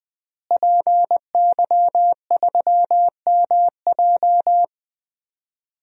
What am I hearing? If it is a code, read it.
Morse code, heard as a keyed tone, about 20 wpm, 700 Hz. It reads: PY3MJ
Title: DXCC-100 Call Signs - 5 Characters - Repeated 2x - 20wpm